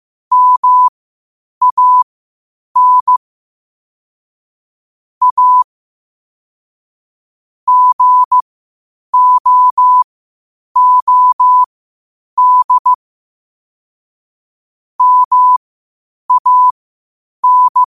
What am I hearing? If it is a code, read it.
Morse code, heard as a keyed tone, about 15 wpm, spaced as for 8 wpm, 1000 Hz. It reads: MAN A GOOD MAN